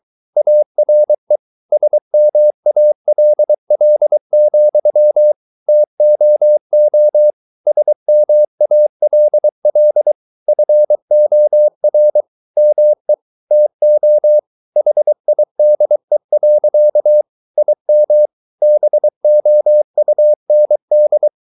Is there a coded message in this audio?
Yes: Morse code from 0.4 to 21.3 s, heard as a keyed tone, about 23 wpm, 600 Hz. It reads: ARE SMALL, TOO SMALL FOR ME TO HIDE. IM BOUND